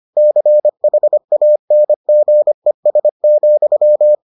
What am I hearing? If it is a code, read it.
Morse code, heard as a keyed tone, about 25 wpm, 600 Hz. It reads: CHANGES,